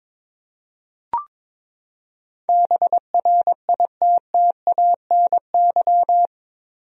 Morse code, heard as a keyed tone, about 21 wpm, 700 Hz. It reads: BRITTANY
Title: USA Common Names - 22wpm